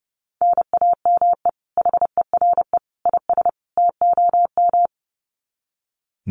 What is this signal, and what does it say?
Morse code, heard as a keyed tone, about 30 wpm, 700 Hz. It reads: NAME HERE IS TOM
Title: Common QSO Phrases 30wpm